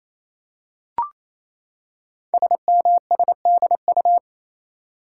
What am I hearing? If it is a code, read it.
Morse code, heard as a keyed tone, about 27 wpm, 700 Hz. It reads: SMSDU
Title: Code Groups - 5 Letters - 28wpm